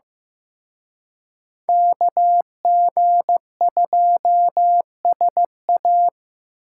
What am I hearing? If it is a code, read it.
Morse code, heard as a keyed tone, about 18 wpm, 700 Hz. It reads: KG2SA
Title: Speed Racing - USA 2x2 Call Signs 15wpm